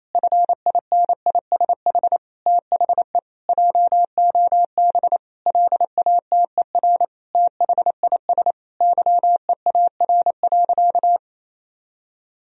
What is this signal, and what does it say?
Morse code, heard as a keyed tone, about 28 wpm, 700 Hz. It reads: FINISH THE JOB LATER THIS YEAR.
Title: Sentences from Top 1000 Words - Review Part 3 - 28wpm